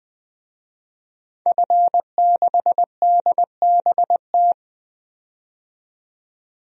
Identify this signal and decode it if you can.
Morse code, heard as a keyed tone, about 20 wpm, 700 Hz. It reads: F6DBT
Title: DXCC-100 Call Signs - 5 Characters - 20wpm